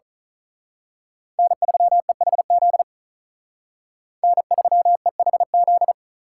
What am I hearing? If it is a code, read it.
Morse code, heard as a keyed tone, about 39 wpm, 700 Hz. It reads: N3EHZ N3EHZ
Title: Speed Racing - USA 1x3 Call Signs 35wpm